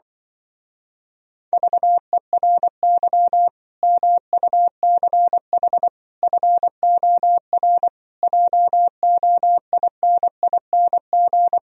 Morse code, heard as a keyed tone, about 24 wpm, 700 Hz. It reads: VERY MUCH FOR JOINING
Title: Speed-Racing - Sets of 5 Words 20wpm